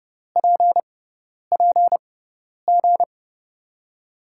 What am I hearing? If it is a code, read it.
Morse code, heard as a keyed tone, about 30 wpm, spaced as for 10 wpm, 700 Hz. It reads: PPG
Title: USA Call Signs - 5 Characters - 6x Character Spacing - 30wpm